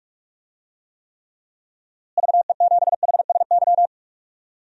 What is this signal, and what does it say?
Morse code, heard as a keyed tone, about 46 wpm, 700 Hz. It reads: VE7HSY
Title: DXCC-100 Call Signs - 6 Characters - Repeated 2x - 45wpm